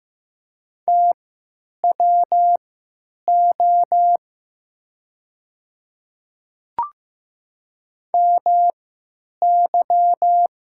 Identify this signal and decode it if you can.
Morse code, heard as a keyed tone, about 15 wpm, spaced as for 9 wpm, 700 Hz. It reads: TWO MY
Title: 3x Character Spacing - Top 100 Words - 15wpm